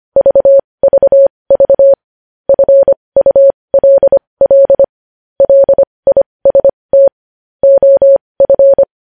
Morse code, heard as a keyed tone, about 25 wpm, spaced as for 20 wpm, 550 Hz. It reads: VVV FULL LIST OF
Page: news-headlines-in-morse-code-at-20-wpm